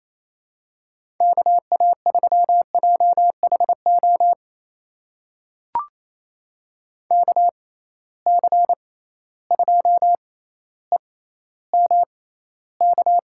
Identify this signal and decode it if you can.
Morse code, heard as a keyed tone, about 27 wpm, spaced as for 14 wpm, 700 Hz. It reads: KA3JHO KC2EMK
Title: USA Call Signs - 6 Characters - 6x Character Spacing - 28wpm